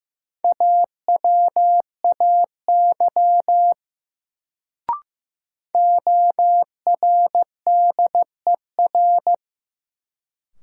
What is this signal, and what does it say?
Morse code, heard as a keyed tone, about 15 wpm, 700 Hz. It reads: AWAY ORDER
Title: Sets of 2 Words - New Word Drill - 15wpm